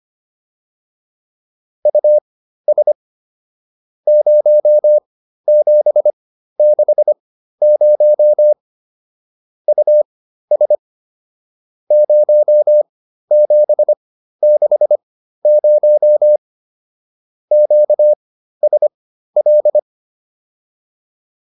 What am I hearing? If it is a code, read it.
Morse code, heard as a keyed tone, about 25 wpm, spaced as for 13 wpm, 600 Hz. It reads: US 0760 US 0760 QSL
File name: POTA207 13